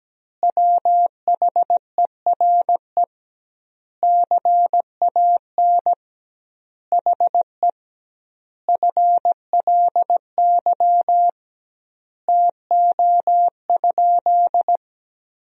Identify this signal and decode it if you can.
Morse code, heard as a keyed tone, about 17 wpm, 700 Hz. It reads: WHERE CAN HE FLY TO?
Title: 2x Word-Spacing - Sentences from Top 500 Words - 17wpm